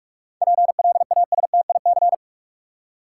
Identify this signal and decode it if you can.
Morse code, heard as a keyed tone, about 46 wpm, 700 Hz. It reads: PLASTIC
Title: Seven Letter Words - 45wpm